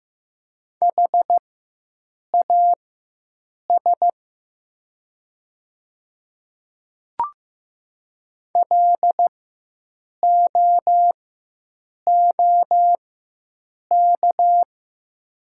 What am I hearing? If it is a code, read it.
Morse code, heard as a keyed tone, about 15 wpm, spaced as for 7 wpm, 700 Hz. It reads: HAS LOOK
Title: 4x Character Spacing - Top 100 Words - 15wpm